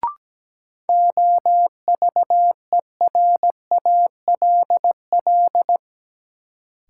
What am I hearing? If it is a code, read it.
Morse code, heard as a keyed tone, about 17 wpm, 700 Hz. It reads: OVERALL
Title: Sentences from Top 1000 Words - Review Part 7 - New Word Drill - 17wpm